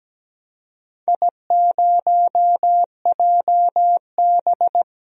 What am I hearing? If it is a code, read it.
Morse code, heard as a keyed tone, about 17 wpm, 700 Hz. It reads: I0JB
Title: DXCC-100 Call Signs - 4 Characters - 17wpm